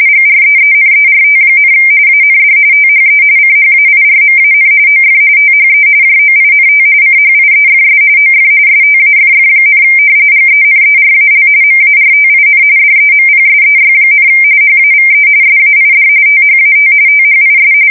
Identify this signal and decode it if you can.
RTTY, about 45.45 baud. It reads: GREENERT, AND THE CHAIRMAN OF THE JOINT CHIEFS OF STAFF, GENERAL MARTIN DEMPSEY, HAVE SAID THAT THIS
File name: fsk170 / usnavy5